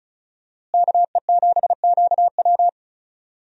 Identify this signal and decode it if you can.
Morse code, heard as a keyed tone, about 36 wpm, 700 Hz. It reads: KE7QW
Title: USA Call Signs - 5 Characters - Repeated 2x - 35wpm